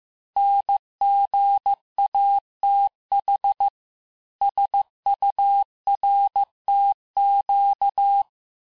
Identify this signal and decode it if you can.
Morse code, heard as a keyed tone, about 15 wpm, 800 Hz. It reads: NGATH SURTQ
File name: cours-du-lundi files / cours-1585-texte-chiffre-vitesse-14mpm